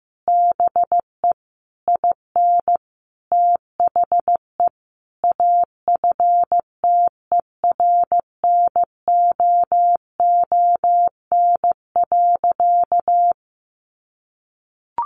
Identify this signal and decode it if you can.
Morse code, heard as a keyed tone, about 15 wpm, 700 Hz. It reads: BE IN THE AFTERNOON.
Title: Sentences from Top 1500 Words 15wpm